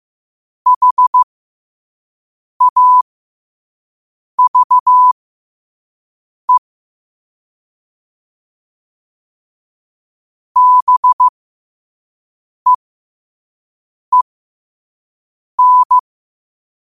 Morse code, heard as a keyed tone, about 15 wpm, spaced as for 5 wpm, 1000 Hz. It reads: HAVE BEEN
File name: qotdfiles / 5wpm